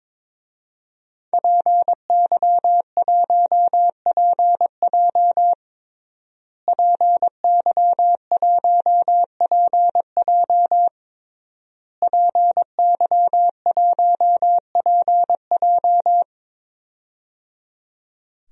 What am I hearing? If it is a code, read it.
Morse code, heard as a keyed tone, about 22 wpm, 700 Hz. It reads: PY1PJ PY1PJ PY1PJ
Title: DXCC-100 Call Signs - 5 Characters - Repeated 3x - 22wpm